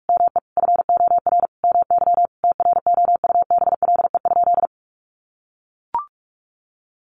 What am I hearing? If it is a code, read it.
Morse code, heard as a keyed tone, about 44 wpm, 700 Hz. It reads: ME FOR MY TROUBLE?
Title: Sentences from Top 1200 Words 45wpm